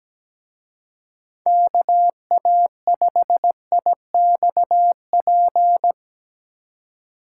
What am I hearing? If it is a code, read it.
Morse code, heard as a keyed tone, about 19 wpm, 700 Hz. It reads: KA5IXP
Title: Speed Racing - USA 2x3 Call Signs 17wpm